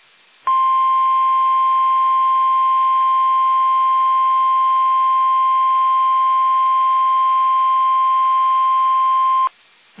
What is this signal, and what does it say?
Emergency Alert System (SAME) tones (the 1050 Hz attention tone). It is heard from 0.4 to 9.5 s.
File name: eas / KUSO / KUSO